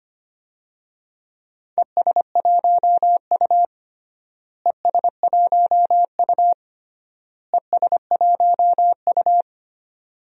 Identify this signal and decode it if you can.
Morse code, heard as a keyed tone, about 25 wpm, 700 Hz. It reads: ES1U ES1U ES1U
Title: DXCC-100 Call Signs - 4 Characters - Repeated 3x - 25wpm